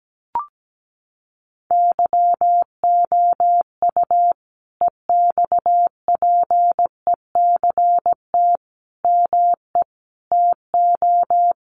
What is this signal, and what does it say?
Morse code, heard as a keyed tone, about 17 wpm, 700 Hz. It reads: YOU EXPECT ME TO
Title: Sentences from Top 1100 Words 17wpm